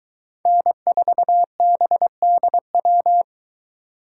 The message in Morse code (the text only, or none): N4BDW